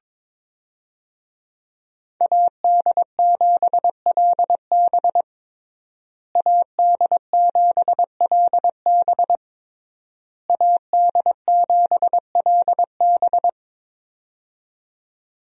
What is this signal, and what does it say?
Morse code, heard as a keyed tone, about 22 wpm, 700 Hz. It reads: AD7LB AD7LB AD7LB
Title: USA Call Signs - 5 Characters - Repeated 3x - 22wpm